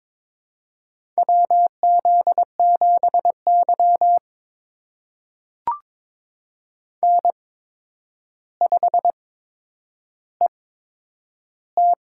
Morse code, heard as a keyed tone, about 22 wpm, spaced as for 9 wpm, 700 Hz. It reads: WZ7Y N5ET
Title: USA Call Signs - 4 Characters - 8x Character Spacing - 22wpm